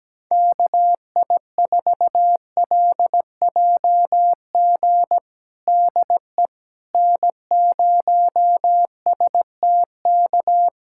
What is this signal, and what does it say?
Morse code, heard as a keyed tone, about 17 wpm, 700 Hz. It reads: KI4LJG DE N0STK
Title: USA Callsign DE Callsign 17wpm